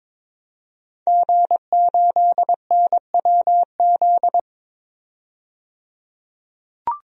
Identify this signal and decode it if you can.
Morse code, heard as a keyed tone, about 22 wpm, 700 Hz. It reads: G8NWZ
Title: DXCC-100 Call Signs - 5 Characters - 22wpm